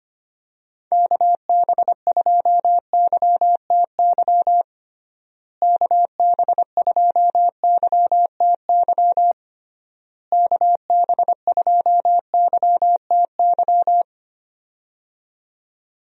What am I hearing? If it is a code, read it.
Morse code, heard as a keyed tone, about 25 wpm, 700 Hz. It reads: KB2YTY KB2YTY KB2YTY
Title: USA Call Signs - 6 Characters - Repeated 3x - 25wpm